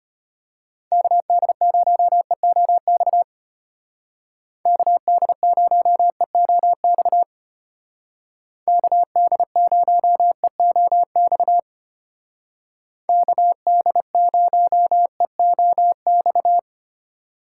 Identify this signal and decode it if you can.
Morse code, heard as a keyed tone, about 33 wpm, 700 Hz. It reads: KD0EOX KD0EOX KD0EOX KD0EOX